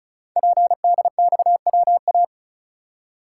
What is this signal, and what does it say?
Morse code, heard as a keyed tone, about 36 wpm, 690 Hz. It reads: PDXWA
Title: Code Groups - 5 Letter:Numbers - 35wpm